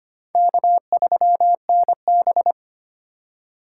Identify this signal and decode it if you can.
Morse code, heard as a keyed tone, about 25 wpm, 700 Hz. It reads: K3NB
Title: USA Call Signs - 4 Characters - 2x Character Spacing - 25wpm